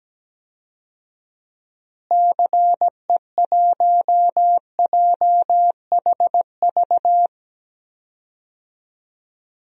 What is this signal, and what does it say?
Morse code, heard as a keyed tone, about 17 wpm, 700 Hz. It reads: CE1JHV
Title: DXCC-100 Call Signs - 6 Characters - Repeated 3x - 17wpm